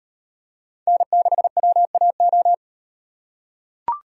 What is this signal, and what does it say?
Morse code, heard as a keyed tone, about 38 wpm, 700 Hz. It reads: N6WAO